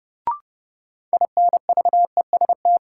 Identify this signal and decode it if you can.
Morse code, heard as a keyed tone, about 30 wpm, 700 Hz. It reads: INVEST
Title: Sentences from Top 1100 Words - New Word Drill Alt - 30wpm